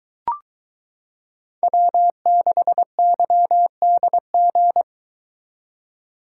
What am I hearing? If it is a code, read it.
Morse code, heard as a keyed tone, about 22 wpm, 700 Hz. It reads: W6YDG